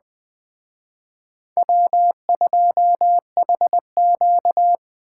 Morse code, heard as a keyed tone, about 20 wpm, 700 Hz. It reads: W2HQ
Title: Speed Racing - USA 1x2 Call Signs 17wpm